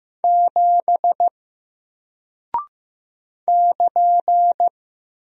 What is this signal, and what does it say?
Morse code, heard as a keyed tone, about 15 wpm, 700 Hz. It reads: 7 <KN>
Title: Single Character (Letter:Number:Punctuation:Prosign) Alt 15wpm